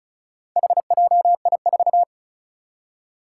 Morse code, heard as a keyed tone, about 36 wpm, 700 Hz. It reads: HJI4
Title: Four Challenging Letter-Number Combinations 35wpm